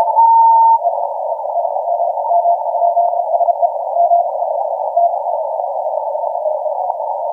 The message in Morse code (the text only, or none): none